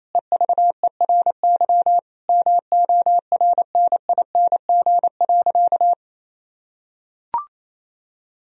Morse code, heard as a keyed tone, about 28 wpm, 700 Hz. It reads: EVERY MORNING.
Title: Sentences from Top 1000 Words - Review Part 2 - 28wpm